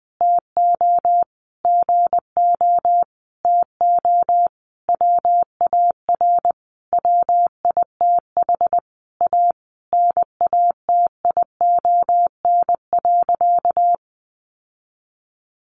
Morse code, heard as a keyed tone, about 20 wpm, 700 Hz. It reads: TO GO TO WAR WITH A NATION.
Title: Sentences from Top 1200 Words 20wpm